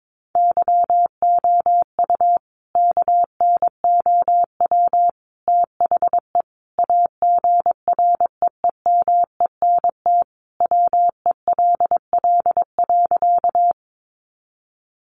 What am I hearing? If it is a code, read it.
Morse code, heard as a keyed tone, about 22 wpm, 700 Hz. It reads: YOU KNOW THE AGREEMENT WELL.
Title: Sentences from Top 1500 Words 22wpm